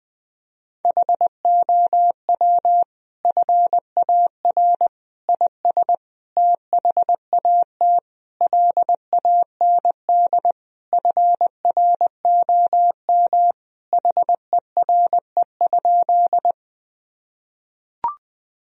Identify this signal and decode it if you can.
Morse code, heard as a keyed tone, about 20 wpm, 700 Hz. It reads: HOW FAR IS THAT LAND FROM HERE?
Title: Sentences from Top 1000 Words - Review Part 1 - 20wpm